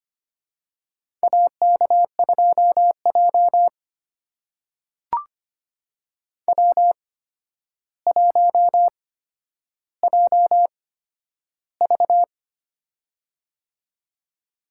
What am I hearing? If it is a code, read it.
Morse code, heard as a keyed tone, about 25 wpm, spaced as for 11 wpm, 700 Hz. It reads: AK2J W1JV
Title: USA Call Signs - 4 Characters - 8x Character Spacing - 25wpm